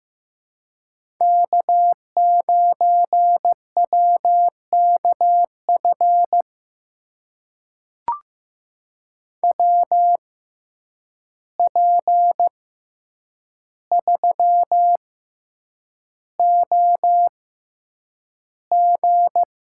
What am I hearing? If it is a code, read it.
Morse code, heard as a keyed tone, about 15 wpm, spaced as for 8 wpm, 700 Hz. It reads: K9WKF WP3OG
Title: USA Call Signs - 5 Characters - 6x Character Spacing - 15wpm